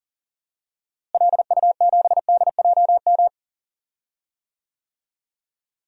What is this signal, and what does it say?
Morse code, heard as a keyed tone, about 40 wpm, 700 Hz. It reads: LU7DJM
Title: DXCC-100 Call Signs - 6 Characters - Repeated 2x - 40wpm